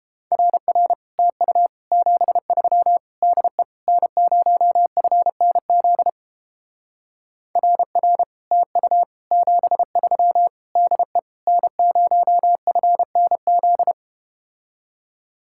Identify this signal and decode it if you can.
Morse code, heard as a keyed tone, about 30 wpm, 700 Hz. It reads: RR TU 73 DE N0FNZ RR TU 73 DE N0FNZ